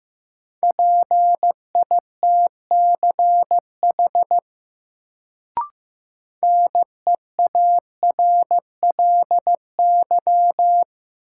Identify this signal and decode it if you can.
Morse code, heard as a keyed tone, about 15 wpm, 700 Hz. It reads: PITCH NEARLY
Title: Sentences from Top 1300 Words - New Word Drill Alt - 15wpm